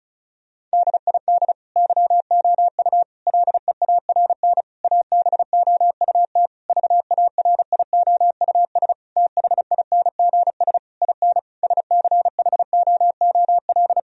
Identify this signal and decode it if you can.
Morse code, heard as a keyed tone, about 34 wpm, 700 Hz. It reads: DID YOU LEARN ABOUT VARIOUS THINGS IN SCHOOL
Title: Sentences from Top 1000 Words - Review Part 4 - 35wpm